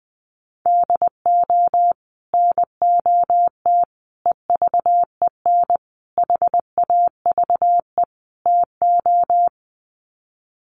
Morse code, heard as a keyed tone, about 20 wpm, 700 Hz. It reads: DO NOT EVEN HAVE TO
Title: Sets of 5 Words Encore2 20wpm